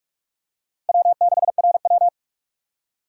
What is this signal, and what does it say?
Morse code, heard as a keyed tone, about 44 wpm, 700 Hz. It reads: W6RW
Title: USA Call Signs - 4 Characters - 2x Character Spacing - 45wpm